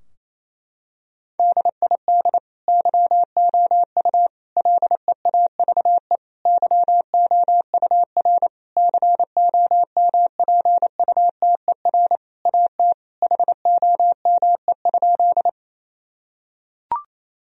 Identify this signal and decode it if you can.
Morse code, heard as a keyed tone, about 28 wpm, 700 Hz. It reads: DID YOU LEAVE YOUR COMPUTER AT HOME?
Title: Sentences from Top 1000 Words - Review Part 3 - 28wpm